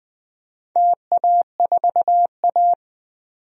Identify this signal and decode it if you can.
Morse code, heard as a keyed tone, about 21 wpm, 700 Hz. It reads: TA4A
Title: DXCC-100 Call Signs - 4 Characters - 20wpm